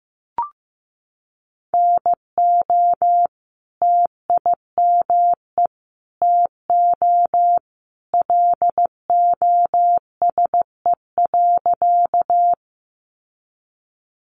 Morse code, heard as a keyed tone, about 15 wpm, 700 Hz. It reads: NO TIME TO LOSE.
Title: Sentences from Top 1100 Words 15wpm